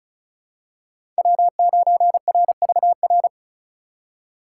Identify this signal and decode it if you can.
Morse code, heard as a keyed tone, about 36 wpm, 700 Hz. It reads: W9RVR
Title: USA Call Signs - 5 Characters - 4x Character Spacing - 35wpm